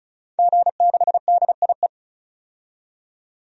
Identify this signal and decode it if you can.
Morse code, heard as a keyed tone, about 36 wpm, 700 Hz. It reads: G6DIE
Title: DXCC-100 Call Signs - 5 Characters - 35wpm